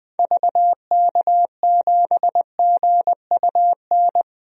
Morse code, heard as a keyed tone, about 20 wpm, 700 Hz. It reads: VK7GUN
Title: DXCC-100 Call Signs - 6 Characters - 20wpm